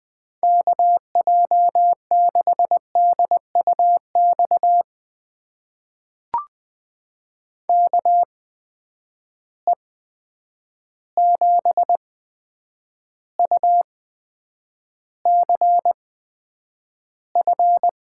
Morse code, heard as a keyed tone, about 20 wpm, spaced as for 9 wpm, 700 Hz. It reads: KJ6DUX KE7UCF
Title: USA Call Signs - 6 Characters - 8x Character Spacing - 20wpm